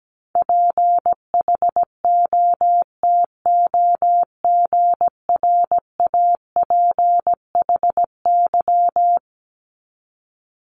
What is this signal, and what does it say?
Morse code, heard as a keyed tone, about 17 wpm, 700 Hz. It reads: PHOTOGRAPHY
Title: Multi-letter Phonemes 17wpm